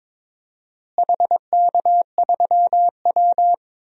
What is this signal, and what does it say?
Morse code, heard as a keyed tone, about 21 wpm, 700 Hz. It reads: HK3W
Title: DXCC-100 Call Signs - 4 Characters - Repeated 3x - 22wpm